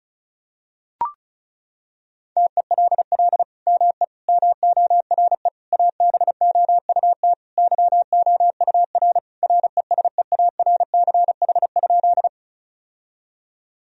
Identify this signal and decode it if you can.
Morse code, heard as a keyed tone, about 34 wpm, 700 Hz. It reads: TELL ME MORE ABOUT YOUR RESEARCH?
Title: Sentences from Top 1000 Words - Review Part 4 - 35wpm